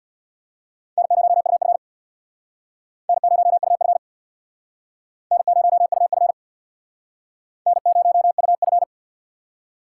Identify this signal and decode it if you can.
Morse code, heard as a keyed tone, about 67 wpm, 700 Hz. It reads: N0UF N0UF N0UF N0UF